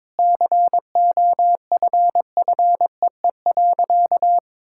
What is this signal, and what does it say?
Morse code, heard as a keyed tone, about 22 wpm, 700 Hz. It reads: COFFEE.
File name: Sentences with 7-Word Phrases - 22wpm